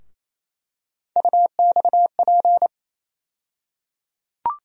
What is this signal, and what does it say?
Morse code, heard as a keyed tone, about 28 wpm, 700 Hz. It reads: UXP
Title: USA Call Sign Postfixes - 3 Letters 28wpm